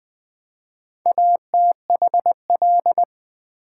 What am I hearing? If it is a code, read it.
Morse code, heard as a keyed tone, about 20 wpm, 700 Hz. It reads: ATHL
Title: Build A Word - 4 Letters At A Time - 8-Letter Words - 20wpm